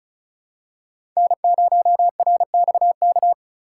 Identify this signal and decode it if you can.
Morse code, heard as a keyed tone, about 36 wpm, 700 Hz. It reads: N0RXK
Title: USA Call Signs - 5 Characters - 2x Character Spacing - 35wpm